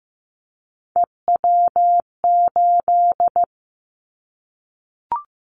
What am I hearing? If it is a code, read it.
Morse code, heard as a keyed tone, about 15 wpm, 700 Hz. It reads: EW8
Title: Callsign Prefixes for Top 50 DXCC - 15wpm